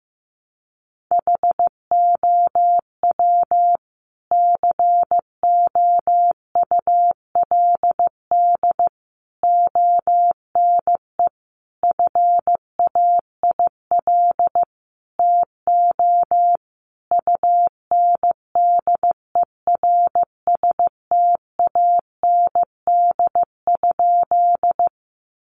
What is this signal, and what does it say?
Morse code, heard as a keyed tone, about 15 wpm, 700 Hz. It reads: HOW COULD ONE FAIL TO UNDERSTAND?
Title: Sentences from Top 1100 Words 15wpm